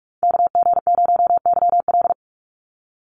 Morse code, heard as a keyed tone, about 44 wpm, 700 Hz. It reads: KG0YL
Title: Call Signs Repeated 2x - 45wpm